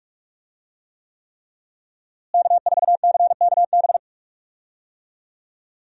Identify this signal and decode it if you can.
Morse code, heard as a keyed tone, about 46 wpm, 700 Hz. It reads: K4CKB
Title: USA Call Signs - 5 Characters - Repeated 3x - 45wpm